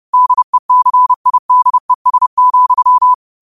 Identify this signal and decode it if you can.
Morse code, heard as a keyed tone, about 30 wpm, 1000 Hz. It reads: DECIDES,
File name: qotdfiles / 30wpm